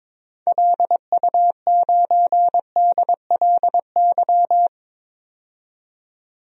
Morse code, heard as a keyed tone, about 22 wpm, 700 Hz. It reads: LU9DLY
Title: DXCC-100 Call Signs - 6 Characters - 22wpm